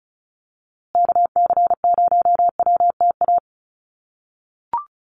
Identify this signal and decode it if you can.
Morse code, heard as a keyed tone, about 34 wpm, 700 Hz. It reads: KC0WTA
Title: USA Call Signs Repeated 3x 35wpm